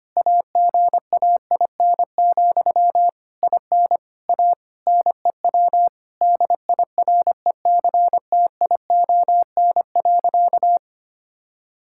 Morse code, heard as a keyed tone, about 25 wpm, 700 Hz. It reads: AGAIN, IN A NEW DIRECTION.
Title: Sentences from Top 1000 Words - Review Part 5 - 25wpm